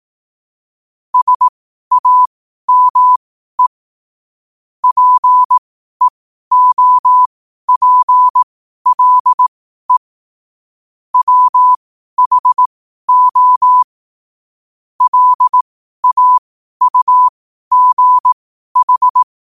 Morse code, heard as a keyed tone, about 18 wpm, spaced as for 12 wpm, 1000 Hz. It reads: SAME PEOPLE WHO LAUGH
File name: qotdfiles / 12wpm